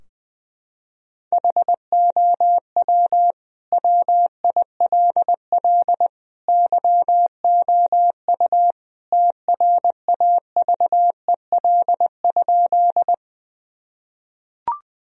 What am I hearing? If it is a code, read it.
Morse code, heard as a keyed tone, about 20 wpm, 700 Hz. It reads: HOW WILL YOU TRAVEL?
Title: Sentences from Top 1000 Words - Review Part 2 - 20wpm